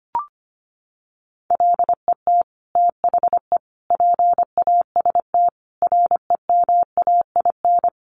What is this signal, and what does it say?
Morse code, heard as a keyed tone, about 25 wpm, 700 Hz. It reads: LET THE PAST REMAIN
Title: Sentences from Top 1200 Words 25wpm